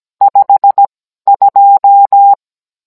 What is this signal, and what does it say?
Morse code, heard as a keyed tone, about 17 wpm, spaced as for 12 wpm, 800 Hz. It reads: 52
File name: Cours-NR18-16mm